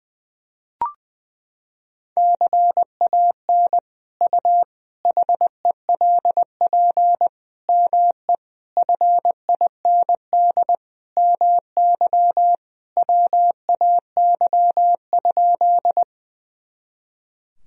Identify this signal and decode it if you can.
Morse code, heard as a keyed tone, about 20 wpm, 700 Hz. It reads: CAN U HELP ME FIND MY WAY?
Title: Sentences from Top 200 Words and Abbreviations 20wpm